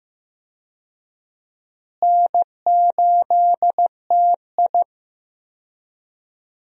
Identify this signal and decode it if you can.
Morse code, heard as a keyed tone, about 15 wpm, 700 Hz. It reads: N8TI
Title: USA Call Signs - 4 Characters - Repeated 2x - 15wpm